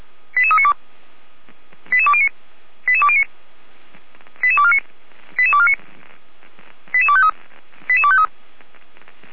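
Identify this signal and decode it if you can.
Five-tone selective call (ZVEI), heard as quick runs of five tones, tones of about 70 ms. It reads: 88392 88209 88209 88249 88249 88253 88253
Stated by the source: ZVEI five-tone call 88228 received by SDR.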